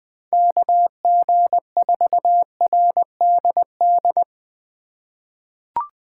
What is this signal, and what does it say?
Morse code, heard as a keyed tone, about 20 wpm, 700 Hz. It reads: KG4RDD
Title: USA Call Signs - 6 Characters - 2x Character Spacing - 20wpm